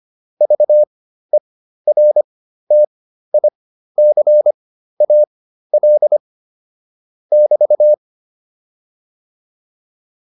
Morse code, heard as a keyed tone, about 25 wpm, spaced as for 13 wpm, 600 Hz. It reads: VERTICAL <BT>
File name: QSO 207 13